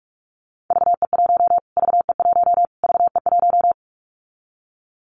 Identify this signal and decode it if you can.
Morse code, heard as a keyed tone, about 44 wpm, 700 Hz. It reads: VE1 VE1 VE1
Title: Callsign Prefixes for Top 25 DXCC - Repeated 3x - 45wpm